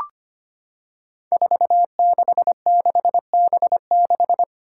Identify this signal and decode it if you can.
Morse code, heard as a keyed tone, about 25 wpm, 700 Hz. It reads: 466B6
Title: Five Challenging Letter-Number Combinations 25wpm